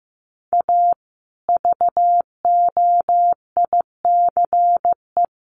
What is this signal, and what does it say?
Morse code, heard as a keyed tone, about 15 wpm, 700 Hz. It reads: A VOICE
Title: Sentences from Top 1200 Words 15wpm